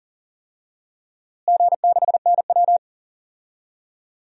Morse code, heard as a keyed tone, about 38 wpm, 700 Hz. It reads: G6NW
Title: DXCC-100 Call Signs - 4 Characters - Repeated 2x - 40wpm